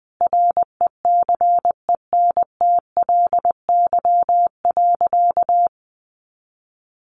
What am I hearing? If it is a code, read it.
Morse code, heard as a keyed tone, about 20 wpm, 700 Hz. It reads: RECENTLY.
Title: Sentences from Top 1200 Words 20wpm